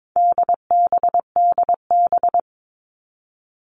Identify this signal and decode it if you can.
Morse code, heard as a keyed tone, about 22 wpm, 700 Hz. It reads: DBDB
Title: All Cut Numbers - 4-Digit 22wpm